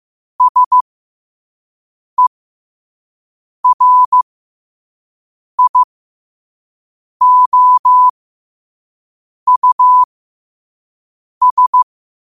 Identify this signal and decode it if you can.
Morse code, heard as a keyed tone, about 15 wpm, spaced as for 5 wpm, 1000 Hz. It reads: SERIOUS
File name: qotdfiles / 5wpm